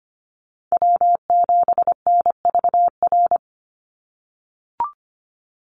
Morse code, heard as a keyed tone, about 25 wpm, 700 Hz. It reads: W7NVR
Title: USA Call Signs 25wpm